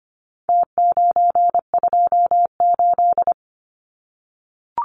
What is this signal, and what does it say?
Morse code, heard as a keyed tone, about 25 wpm, 700 Hz. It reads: T928
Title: Numbers - Contest Style - 4-Digit 25wpm